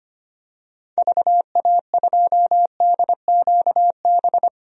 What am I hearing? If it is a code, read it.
Morse code, heard as a keyed tone, about 25 wpm, 700 Hz. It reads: VA2DQB
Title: DXCC-100 Call Signs - 6 Characters - Repeated 3x - 25wpm